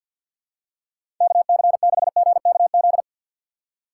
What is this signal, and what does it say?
Morse code, heard as a keyed tone, about 51 wpm, 690 Hz. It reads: KC6GKZ
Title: USA Call Signs - 6 Characters - 3x Character Spacing - 50wpm